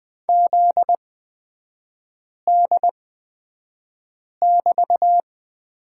Morse code, heard as a keyed tone, about 20 wpm, spaced as for 5 wpm, 700 Hz. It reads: ZD<BT>
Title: 243 Lesson 39e - Letter z - Three Characters 20wpm